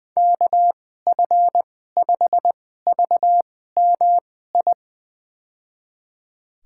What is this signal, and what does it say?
Morse code, heard as a keyed tone, about 20 wpm, spaced as for 14 wpm, 700 Hz. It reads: KF5VMI
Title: USA Call Signs - 6 Characters - 2x Character Spacing - 20wpm